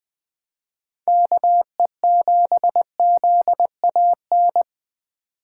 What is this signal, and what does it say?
Morse code, heard as a keyed tone, about 20 wpm, 700 Hz. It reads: KE7ZAN
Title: USA Call Signs - 2x Character Spacing - 20wpm